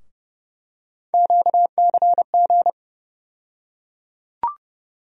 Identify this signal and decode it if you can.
Morse code, heard as a keyed tone, about 30 wpm, 700 Hz. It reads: QCG
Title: USA Call Sign Postfixes - 3 Letters 30wpm